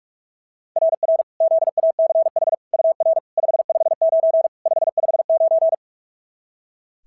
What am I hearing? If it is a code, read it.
Morse code, heard as a keyed tone, about 41 wpm, 690 Hz. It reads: RR ZACH UR 559 559
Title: SOTA QSO Practice 45wpm